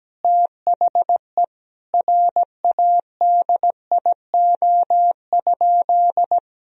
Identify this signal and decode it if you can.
Morse code, heard as a keyed tone, about 17 wpm, 700 Hz. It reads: THE RADIO?
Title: Sentences from Top 1000 Words - Review Part 3 - 17wpm